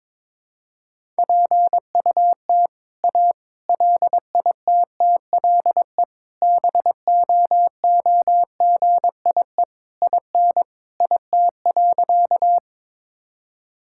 Morse code, heard as a keyed tone, about 22 wpm, 700 Hz. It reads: PUT A LITTLE BOOGIE IN IT.
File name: All-Age Appropriate Jokes - 22wpm